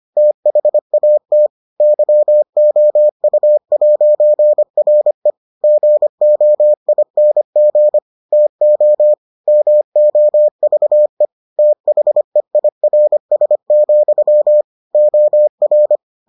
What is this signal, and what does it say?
Morse code, heard as a keyed tone, about 25 wpm, 600 Hz. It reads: THAT YOU'RE GOING TO MOVE THEIRS, OR